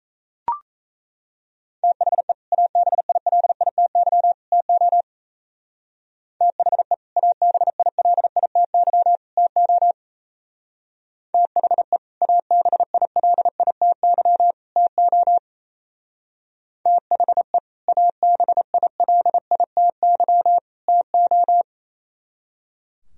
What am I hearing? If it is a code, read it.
Morse code, heard as a keyed tone, about 35 wpm, 700 Hz. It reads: THE ABILITY TO THE ABILITY TO THE ABILITY TO THE ABILITY TO
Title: Speed-Racing - Sets of 3 Words 28wpm